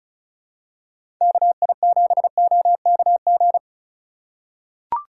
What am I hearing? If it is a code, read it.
Morse code, heard as a keyed tone, about 36 wpm, 700 Hz. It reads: KI7OKG
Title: USA Call Signs - 6 Characters - 2x Character Spacing - 35wpm